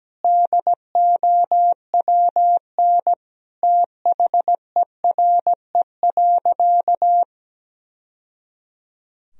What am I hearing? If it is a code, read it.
Morse code, heard as a keyed tone, about 17 wpm, 700 Hz. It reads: DOWN THERE.
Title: Sentences from Top 1000 Words - Review Part 4 - 17wpm